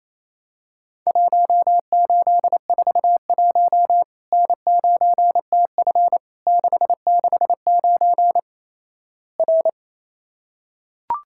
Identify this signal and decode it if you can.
Morse code, heard as a keyed tone, about 28 wpm, 700 Hz. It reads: 1841 N9TF 669 R
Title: Worked All Europe DX Contest - QTC Exchange - 28wpm